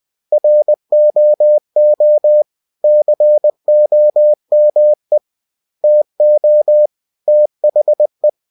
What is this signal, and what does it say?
Morse code, heard as a keyed tone, about 20 wpm, 600 Hz. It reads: ROO COME TO THE